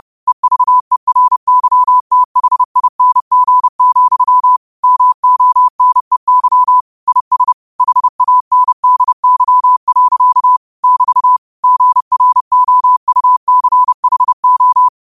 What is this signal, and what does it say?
Morse code, heard as a keyed tone, about 30 wpm, 1000 Hz. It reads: EVERYTHING, MONEY IS HANDY. <BT> GROUCHO
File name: qotdfiles / 30wpm